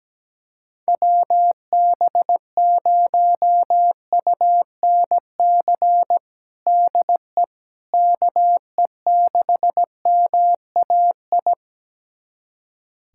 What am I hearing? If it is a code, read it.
Morse code, heard as a keyed tone, about 17 wpm, 700 Hz. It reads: WB0UNC DE KE6MAI
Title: USA Callsign DE Callsign 17wpm